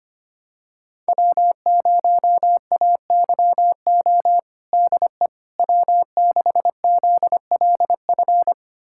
Morse code, heard as a keyed tone, about 25 wpm, 700 Hz. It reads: W0AYO DE W6ZLF
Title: USA Callsign DE Callsign - 5 Characters - 25wpm